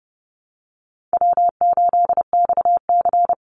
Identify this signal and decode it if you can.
Morse code, heard as a keyed tone, about 30 wpm, 700 Hz. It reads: W8XC
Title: Easy Call Signs Repeated 2x - 30wpm